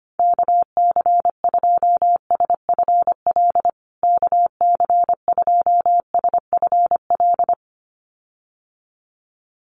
Morse code, heard as a keyed tone, about 25 wpm, 700 Hz. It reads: KC2SFL KC2SFL
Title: USA Call Signs Repeated 2x 25wpm